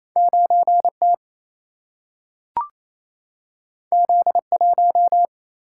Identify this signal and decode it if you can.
Morse code, heard as a keyed tone, about 28 wpm, 700 Hz. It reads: 9T Z1
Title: Single-Letter Single-Number 28wpm